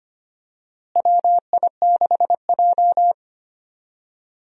Morse code, heard as a keyed tone, about 25 wpm, 700 Hz. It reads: WI6J
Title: USA Call Signs - 4 Characters - Repeated 2x - 25wpm